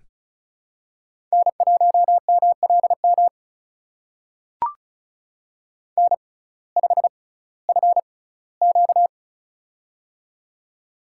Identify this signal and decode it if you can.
Morse code, heard as a keyed tone, about 36 wpm, spaced as for 19 wpm, 700 Hz. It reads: N1MLM N5FQ
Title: USA Call Signs - 6x Character Spacing - 35wpm